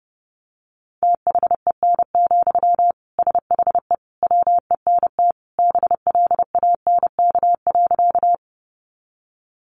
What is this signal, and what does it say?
Morse code, heard as a keyed tone, about 30 wpm, 700 Hz. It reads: THEN, SHE WENT BLANK.
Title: Sentences from Top 1500 Words 30wpm